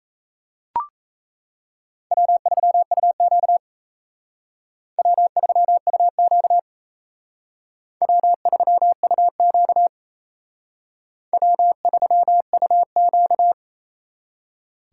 Morse code, heard as a keyed tone, about 35 wpm, 700 Hz. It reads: W3UQ W3UQ W3UQ W3UQ